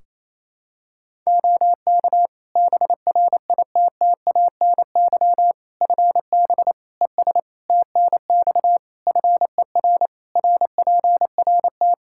Morse code, heard as a keyed tone, about 28 wpm, 700 Hz. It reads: OK BRITTANY FB ES TNX FER RPRT
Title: Practice QSO - Exchange 2 - Info - 28wpm